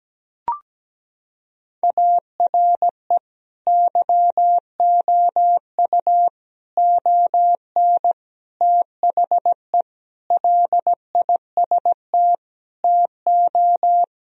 Morse code, heard as a keyed tone, about 17 wpm, 700 Hz. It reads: ARE YOU ON THE LIST TO
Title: Sentences from Top 1000 Words - Review Part 2 - 17wpm